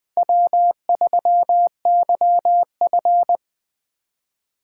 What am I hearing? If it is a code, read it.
Morse code, heard as a keyed tone, about 21 wpm, 700 Hz. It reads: W3YF
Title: USA Call Signs - 4 Characters - 3x Character Spacing - 20wpm